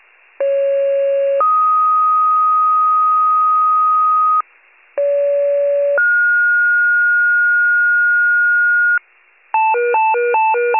Two-tone page, heard as a one-second tone, then a three-second tone, about 550 then 1250 Hz, about 570 then 1430 Hz, with noise. It is heard from 0.4 to 9.0 s.
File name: wcalert